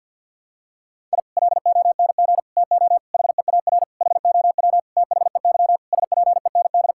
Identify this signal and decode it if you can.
Morse code, heard as a keyed tone, about 46 wpm, 700 Hz. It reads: I LONG TO HEAR HOW THEY SPEND
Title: Sentences from Top 1000 Words - Review Part 5 - 50wpm